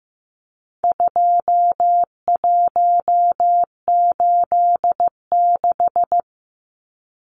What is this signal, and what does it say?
Morse code, heard as a keyed tone, about 15 wpm, 700 Hz. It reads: 2186
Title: Numbers - 4-Digit 15wpm